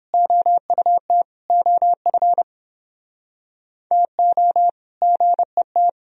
Morse code, heard as a keyed tone, about 28 wpm, 700 Hz. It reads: OUT OF TO GET